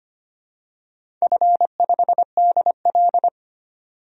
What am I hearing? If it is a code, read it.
Morse code, heard as a keyed tone, about 25 wpm, 700 Hz. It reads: F5DL
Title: DXCC-100 Call Signs - 4 Characters - 25wpm